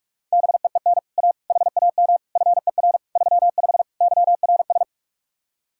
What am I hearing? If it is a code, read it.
Morse code, heard as a keyed tone, about 44 wpm, 700 Hz. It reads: BEEN A HAM FER 35 YRS